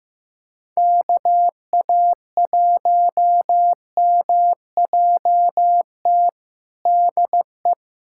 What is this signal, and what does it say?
Morse code, heard as a keyed tone, about 16 wpm, 700 Hz. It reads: KA1MJT DE
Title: USA Callsign DE Callsign - 6 Characters - 15wpm